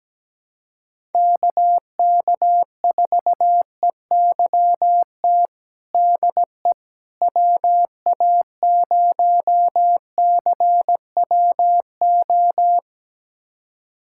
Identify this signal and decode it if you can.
Morse code, heard as a keyed tone, about 17 wpm, 700 Hz. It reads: KK4EYT DE WA0CWO
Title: USA Callsign DE Callsign - 6 Characters - 17wpm